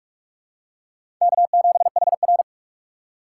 Morse code, heard as a keyed tone, about 46 wpm, 690 Hz. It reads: K7HR
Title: USA Call Signs - 4 Characters - 2x Character Spacing - 45wpm